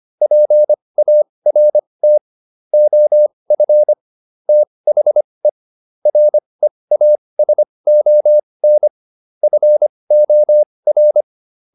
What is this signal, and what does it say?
Morse code, heard as a keyed tone, about 25 wpm, spaced as for 20 wpm, 600 Hz. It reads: PART OF THE REASON FOR